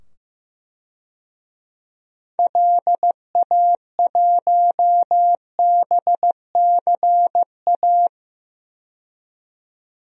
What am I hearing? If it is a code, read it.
Morse code, heard as a keyed tone, about 15 wpm, 700 Hz. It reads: LA1BCA
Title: DXCC-100 Call Signs - 6 Characters - 15wpm